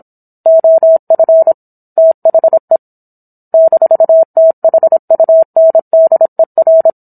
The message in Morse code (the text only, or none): OF THE -THUNDER